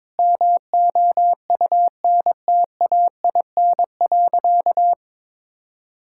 Morse code, heard as a keyed tone, about 21 wpm, 700 Hz. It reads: MOUNTAIN.
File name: practice / 22wpm / Sentences with 3-Word Phrases - 22wpm